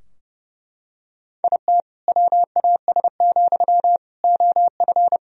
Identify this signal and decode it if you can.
Morse code, heard as a keyed tone, about 30 wpm, 700 Hz. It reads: IT WAS, OF
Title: Sentences from Top 1000 Words - Review Part 3 - 30wpm